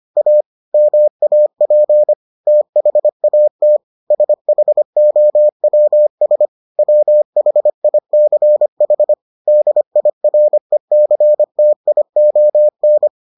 Morse code, heard as a keyed tone, about 25 wpm, 600 Hz. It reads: A MAP THAT SHOWS WHICH DIRECTION